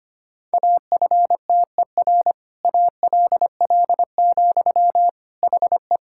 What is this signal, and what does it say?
Morse code, heard as a keyed tone, about 25 wpm, 700 Hz. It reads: AFTER ALL, HE
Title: Sentences from Top 1600 Words 25wpm